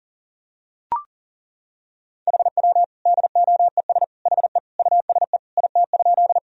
Morse code, heard as a keyed tone, about 37 wpm, 700 Hz. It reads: HW DOES HE USE IT?
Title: Sentences from Top 200 Words and Abbreviations 40wpm